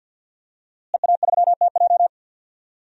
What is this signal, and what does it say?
Morse code, heard as a keyed tone, about 51 wpm, 700 Hz. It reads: EA3TJ
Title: DXCC-100 Call Signs - 5 Characters - Repeated 3x - 50wpm